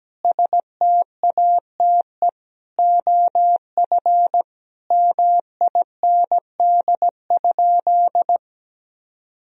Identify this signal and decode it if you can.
Morse code, heard as a keyed tone, about 17 wpm, 700 Hz. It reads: STATE OF MIND?
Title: Sentences from Top 1000 Words - Review Part 2 - 17wpm